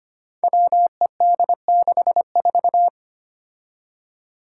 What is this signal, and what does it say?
Morse code, heard as a keyed tone, about 25 wpm, 700 Hz. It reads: WED64
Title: Five Challenging Letter-Number Combinations 25wpm